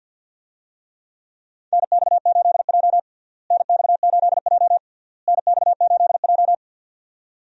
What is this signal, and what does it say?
Morse code, heard as a keyed tone, about 49 wpm, 700 Hz. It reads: NX8J NX8J NX8J
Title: USA Call Signs - 4 Characters - Repeated 3x - 50wpm